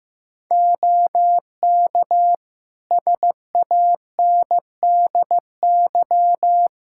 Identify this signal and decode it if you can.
Morse code, heard as a keyed tone, about 15 wpm, 700 Hz. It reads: OK SANDY
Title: Practice QSO - Exchange 3 - More Info - 15wpm